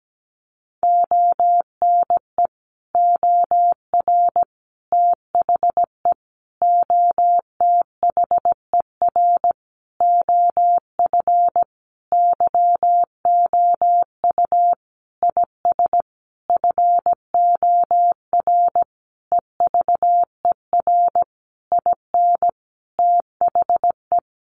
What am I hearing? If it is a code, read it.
Morse code, heard as a keyed tone, about 17 wpm, 700 Hz. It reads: ONE OR THE OTHER OF YOU IS FOR EVER IN THE
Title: Sentences from Top 400 Words 17wpm